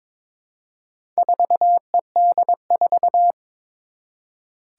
Morse code, heard as a keyed tone, about 22 wpm, 700 Hz. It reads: 4ED4
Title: Four Challenging Letter-Number Combinations 22wpm